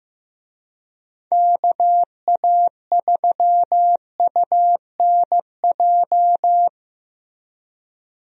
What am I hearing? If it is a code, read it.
Morse code, heard as a keyed tone, about 16 wpm, 700 Hz. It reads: KA3UNJ
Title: USA Call Signs - 6 Characters - Repeated 3x - 15wpm